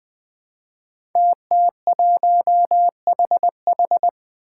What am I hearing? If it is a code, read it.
Morse code, heard as a keyed tone, about 20 wpm, 700 Hz. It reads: TT1HH
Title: Five Challenging Letter-Number Combinations 20wpm